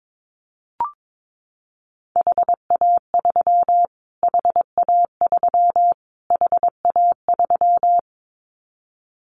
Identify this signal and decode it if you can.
Morse code, heard as a keyed tone, about 22 wpm, 700 Hz. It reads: HA3 HA3 HA3
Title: Callsign Prefixes for Top 25 DXCC - Repeated 3x - 22wpm